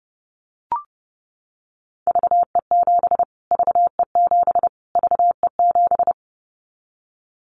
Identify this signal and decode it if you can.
Morse code, heard as a keyed tone, about 30 wpm, 700 Hz. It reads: VE7 VE7 VE7
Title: Callsign Prefixes for Top 25 DXCC - Repeated 3x - 30wpm